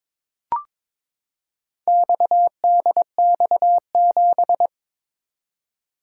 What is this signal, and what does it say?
Morse code, heard as a keyed tone, about 22 wpm, 700 Hz. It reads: XDX7
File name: practice / 22wpm / Four Letter-Number Combinations - 22wpm